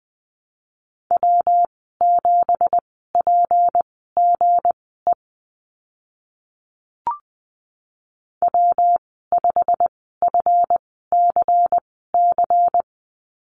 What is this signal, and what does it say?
Morse code, heard as a keyed tone, about 19 wpm, spaced as for 14 wpm, 700 Hz. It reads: W7PGE W5FCC
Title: 173 Lesson 28g - Number 9 - Call Signs 20wpm